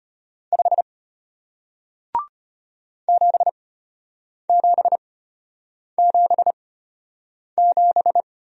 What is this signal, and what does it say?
Morse code, heard as a keyed tone, about 34 wpm, spaced as for 8 wpm, 700 Hz. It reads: 5 7777